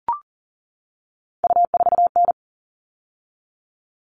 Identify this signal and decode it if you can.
Morse code, heard as a keyed tone, about 40 wpm, 700 Hz. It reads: U4N
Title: All Cut Numbers - 3-Digit 40wpm